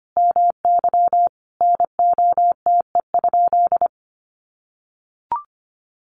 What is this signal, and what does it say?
Morse code, heard as a keyed tone, about 25 wpm, 700 Hz. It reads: MY NOTE?
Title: Sentences from Top 500 Words 25wpm